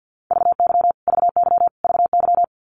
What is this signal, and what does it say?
Morse code, heard as a keyed tone, about 50 wpm, 700 Hz. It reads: VY VY VY
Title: Abbreviations Repeated 3x - 50wpm